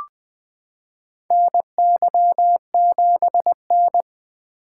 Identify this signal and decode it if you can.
Morse code, heard as a keyed tone, about 21 wpm, 700 Hz. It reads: NY7N
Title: K1USN SST QSO Practice - 20wpm